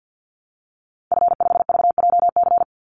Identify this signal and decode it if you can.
Morse code, heard as a keyed tone, about 50 wpm, 700 Hz. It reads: F5VJC
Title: Call Signs - 50wpm